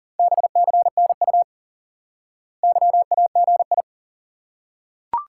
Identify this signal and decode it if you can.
Morse code, heard as a keyed tone, about 40 wpm, 700 Hz. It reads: BCNU YAGI
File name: IQE Course 3 - Lesson 89 - YAGI - Two Elements - 40wpm